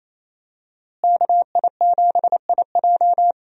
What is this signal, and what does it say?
Morse code, heard as a keyed tone, about 28 wpm, 700 Hz. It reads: KI7IJ
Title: USA Call Signs - 5 Characters - Repeated 3x - 28wpm